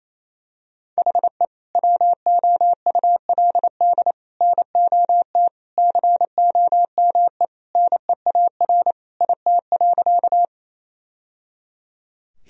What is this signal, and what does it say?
Morse code, heard as a keyed tone, about 28 wpm, 700 Hz. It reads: HE WOULD NOT COME NEAR IT.
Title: Sentences from Top 1000 Words - Review Part 1 - 28wpm